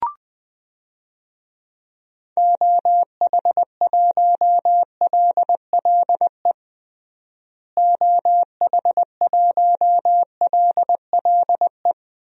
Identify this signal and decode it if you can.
Morse code, heard as a keyed tone, about 20 wpm, 700 Hz. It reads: OH1LLE OH1LLE
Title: DXCC-100 Call Signs - 6 Characters - Repeated 2x - 20wpm